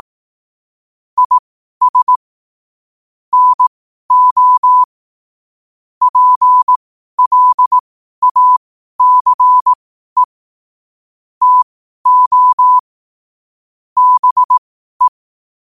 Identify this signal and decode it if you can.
Morse code, heard as a keyed tone, about 18 wpm, spaced as for 12 wpm, 1000 Hz. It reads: IS NO PLACE TO BE